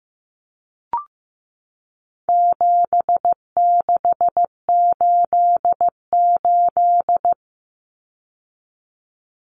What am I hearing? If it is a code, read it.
Morse code, heard as a keyed tone, about 15 wpm, 700 Hz. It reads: 7688
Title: Common Cut Numbers - 4-Digit 15wpm